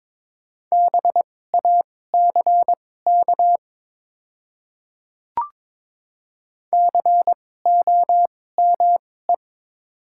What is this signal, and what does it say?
Morse code, heard as a keyed tone, about 22 wpm, spaced as for 16 wpm, 700 Hz. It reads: BACK COME
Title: 2x Character Spacing - Top 200 Words - 22wpm